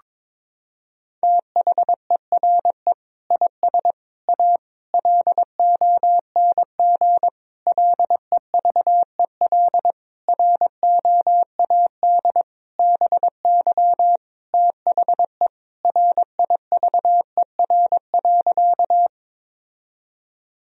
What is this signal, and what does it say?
Morse code, heard as a keyed tone, about 22 wpm, 700 Hz. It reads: THERE IS A LONG LEVEL ROAD BY THE RIVER.
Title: Sentences from Top 1000 Words - Review Part 4 - 22wpm